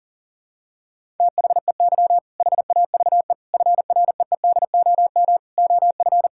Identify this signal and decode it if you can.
Morse code, heard as a keyed tone, about 40 wpm, 700 Hz. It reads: THEY HAVE FREEDOM OF
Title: Sentences from Top 1000 Words - Review Part 5 - 40wpm